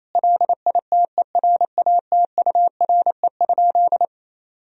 Morse code, heard as a keyed tone, about 28 wpm, 700 Hz. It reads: LITERATURE?
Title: Sentences from Top 1000 Words - Review Part 3 - 28wpm